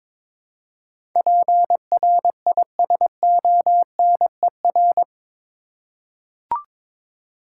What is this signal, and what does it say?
Morse code, heard as a keyed tone, about 21 wpm, 700 Hz. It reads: PRISONER
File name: practice / 22wpm / Eight Letter Words - 22wpm